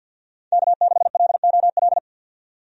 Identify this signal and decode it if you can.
Morse code, heard as a keyed tone, about 46 wpm, 700 Hz. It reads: K6LOL
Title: USA Call Signs - 5 Characters - 50wpm